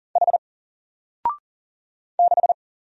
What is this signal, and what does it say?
Morse code, heard as a keyed tone, about 40 wpm, 700 Hz. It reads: H 6
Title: Single Letter-Number 40wpm